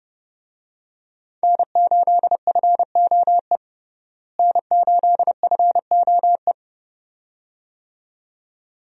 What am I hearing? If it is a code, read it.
Morse code, heard as a keyed tone, about 32 wpm, 700 Hz. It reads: N8FOE N8FOE
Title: USA Call Signs - 5 Characters - Repeated 2x - 30wpm